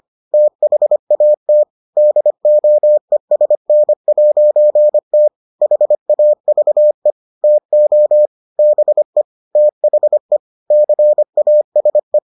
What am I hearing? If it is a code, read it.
Morse code, heard as a keyed tone, about 25 wpm, 600 Hz. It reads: THAT DOESN'T HAVE TO BE THE CASE